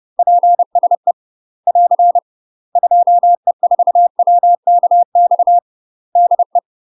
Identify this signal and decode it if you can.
Morse code, heard as a keyed tone, about 30 wpm, 700 Hz. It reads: PSE <AR> 2E4WKX DE
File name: UK MASTER QSOs TRAINING / 30WPM / UK MASTER QSOs 0038